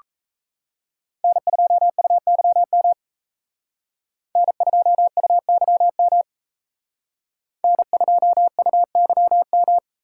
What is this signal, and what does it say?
Morse code, heard as a keyed tone, about 35 wpm, 700 Hz. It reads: N2UYM N2UYM N2UYM